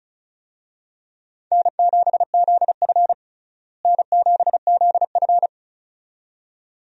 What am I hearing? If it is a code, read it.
Morse code, heard as a keyed tone, about 36 wpm, 700 Hz. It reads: N7ZF N7ZF
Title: USA Call Signs - 4 Characters - Repeated 2x - 35wpm